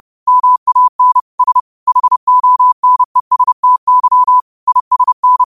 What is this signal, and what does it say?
Morse code, heard as a keyed tone, about 30 wpm, 1000 Hz. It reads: MAN S HONESTY ISN
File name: qotdfiles / 30wpm